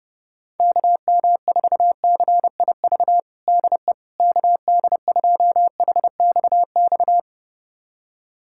Morse code, heard as a keyed tone, about 32 wpm, 700 Hz. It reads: KM4CIV DE KD2HXX
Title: USA Callsign DE Callsign 30wpm